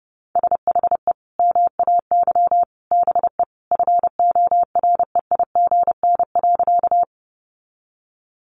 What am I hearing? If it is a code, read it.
Morse code, heard as a keyed tone, about 30 wpm, 700 Hz. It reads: SHE MAY BE FOREIGN.
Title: Sentences from Top 1400 Words 30wpm